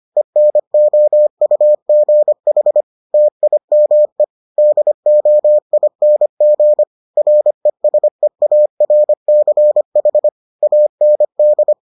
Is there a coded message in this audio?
Yes: Morse code from 0.2 to 11.7 s, heard as a keyed tone, about 25 wpm, 600 Hz. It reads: ENOUGH TIME DOING RESEARCH AND